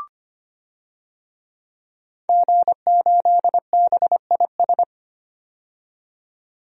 Morse code, heard as a keyed tone, about 25 wpm, 700 Hz. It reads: G8BIS
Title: DXCC-100 Call Signs - 5 Characters - 25wpm